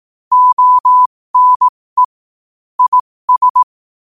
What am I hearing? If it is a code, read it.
Morse code, heard as a keyed tone, about 18 wpm, spaced as for 15 wpm, 1000 Hz. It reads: ONE IS
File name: qotdfiles / 15wpm